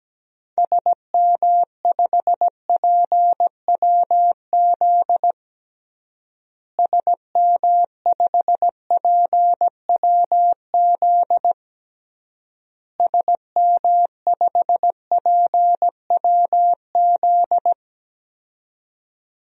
Morse code, heard as a keyed tone, about 17 wpm, 700 Hz. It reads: SM5PWZ SM5PWZ SM5PWZ
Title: DXCC-100 Call Signs - 6 Characters - Repeated 3x - 17wpm